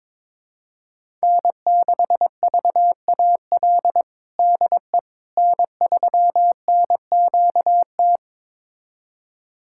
Morse code, heard as a keyed tone, about 22 wpm, 700 Hz. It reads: N6VAL DE N3NQT
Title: USA Callsign DE Callsign - 5 Characters - 22wpm